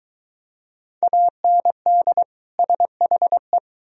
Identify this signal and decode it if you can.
Morse code, heard as a keyed tone, about 23 wpm, 700 Hz. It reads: AND SHE